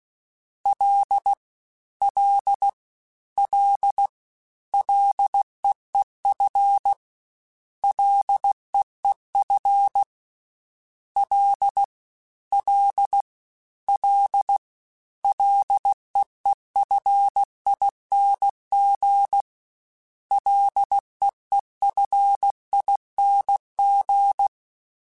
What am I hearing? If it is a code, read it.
Morse code, heard as a keyed tone, about 16 wpm, 800 Hz. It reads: L L L LEEF LEEF L L L LEEFING LEEFING
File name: Ls / Single L m3s / Single L @15